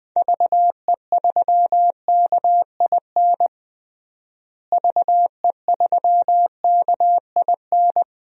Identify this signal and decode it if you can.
Morse code, heard as a keyed tone, about 20 wpm, 700 Hz. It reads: VE3KIN VE3KIN
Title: DXCC-100 Call Signs - 6 Characters - Repeated 3x - 20wpm